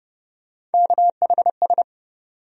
Morse code, heard as a keyed tone, about 30 wpm, 700 Hz. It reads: KHS
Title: USA Call Sign Postfixes - 3 Letters 30wpm